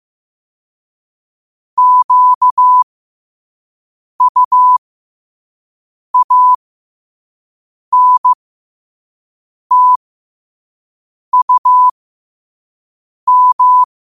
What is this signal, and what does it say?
Morse code, heard as a keyed tone, about 15 wpm, spaced as for 5 wpm, 1000 Hz. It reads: QUANTUM
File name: qotdfiles / 5wpm